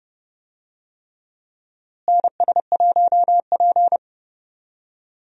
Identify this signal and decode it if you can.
Morse code, heard as a keyed tone, about 30 wpm, 700 Hz. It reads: NS1P
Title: USA Call Signs - 4 Characters - Repeated 3x - 30wpm